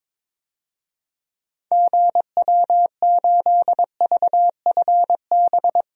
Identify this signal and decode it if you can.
Morse code, heard as a keyed tone, about 22 wpm, 700 Hz. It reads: GW8VFB
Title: DXCC-100 Call Signs - 6 Characters - 22wpm